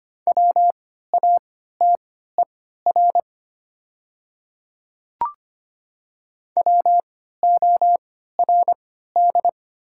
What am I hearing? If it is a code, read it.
Morse code, heard as a keyed tone, about 25 wpm, spaced as for 14 wpm, 700 Hz. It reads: WATER WORD
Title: 3x Character Spacing - Top 100 Words - 25wpm